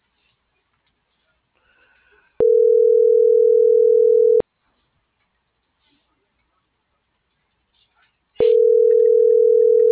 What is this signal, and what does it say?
Ringback tone (North America), 440 + 480 Hz.
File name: CENTRAL MAERSK / out isabel.bravo - .84108